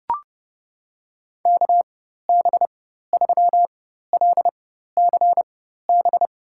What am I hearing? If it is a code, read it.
Morse code, heard as a keyed tone, about 32 wpm, spaced as for 14 wpm, 700 Hz. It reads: KB3LCB
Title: USA Call Signs - 6 Characters - 4x Character Spacing - 30wpm